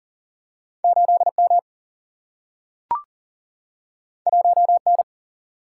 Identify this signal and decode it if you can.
Morse code, heard as a keyed tone, about 40 wpm, 700 Hz. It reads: 8 M 1 N
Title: Single-Letter Single-Number 40wpm